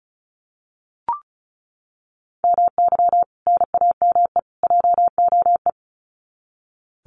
Morse code, heard as a keyed tone, about 34 wpm, 700 Hz. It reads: MY NAME JOE
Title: Common QSO Phrases 35wpm